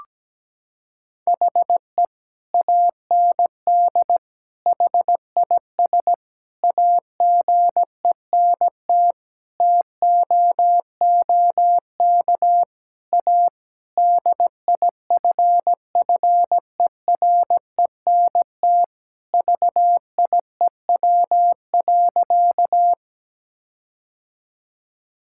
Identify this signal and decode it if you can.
Morse code, heard as a keyed tone, about 17 wpm, 700 Hz. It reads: HE AND HIS AGENT TOOK A DIFFERENT VIEW.
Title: Sentences from Top 1600 Words 17wpm